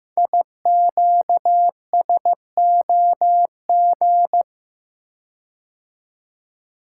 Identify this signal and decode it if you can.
Morse code, heard as a keyed tone, about 15 wpm, 700 Hz. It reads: IQSOG
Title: Code Groups - 5 Letters - 15wpm